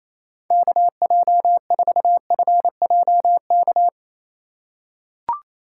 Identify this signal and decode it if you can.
Morse code, heard as a keyed tone, about 28 wpm, 700 Hz. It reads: KJ4FJK
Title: USA Call Signs - 8x Character Spacing - 28wpm